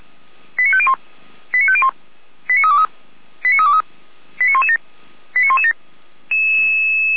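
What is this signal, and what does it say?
Five-tone selective call (ZVEI), heard as quick runs of five tones, tones of about 70 ms. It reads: 89501 89501 89323 89323 89117 89117
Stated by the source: Five-tone ZVEI selective call, code 89512, received by SDR.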